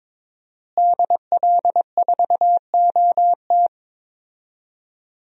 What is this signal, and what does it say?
Morse code, heard as a keyed tone, about 21 wpm, 700 Hz. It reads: DL4OT
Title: USA Call Signs - 5 Characters - Repeated 3x - 22wpm